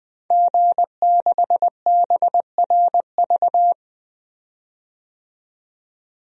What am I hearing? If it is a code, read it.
Morse code, heard as a keyed tone, about 20 wpm, 700 Hz. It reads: G6BRV
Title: DXCC-100 Call Signs - 5 Characters - 20wpm